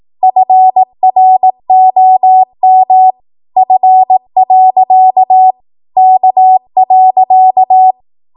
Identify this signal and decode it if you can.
Morse code, heard as a keyed tone, about 18 wpm, 750 Hz. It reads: FROM F. K.
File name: Archive / 18 WPM / 18WPM